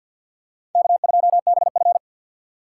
Morse code, heard as a keyed tone, about 51 wpm, 700 Hz. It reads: K2BF
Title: USA Call Signs - 4 Characters - 2x Character Spacing - 50wpm